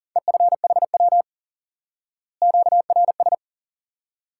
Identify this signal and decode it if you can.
Morse code, heard as a keyed tone, about 40 wpm, 700 Hz. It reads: EFHW QRS
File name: IQE Course 3 - Lesson 89 - YAGI - Two Elements - 40wpm